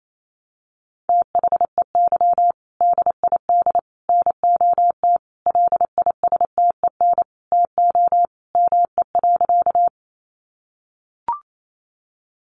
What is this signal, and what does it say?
Morse code, heard as a keyed tone, about 28 wpm, 700 Hz. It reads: THEY DID NOT LISTEN TO ME.
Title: Sentences from Top 600 Words 28wpm